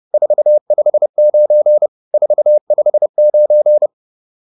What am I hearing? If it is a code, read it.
Morse code, heard as a keyed tone, about 30 wpm, 600 Hz. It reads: 459 459
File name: QSO406 30